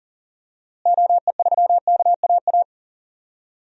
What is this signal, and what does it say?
Morse code, heard as a keyed tone, about 40 wpm, 700 Hz. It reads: OE3KAA
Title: DXCC-100 Call Signs - 6 Characters - Repeated 2x - 40wpm